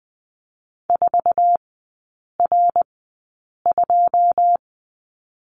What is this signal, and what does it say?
Morse code, heard as a keyed tone, about 20 wpm, spaced as for 8 wpm, 690 Hz. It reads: 4R2
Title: 133 Lesson 22e - Question Mark - Three Characters 20wpm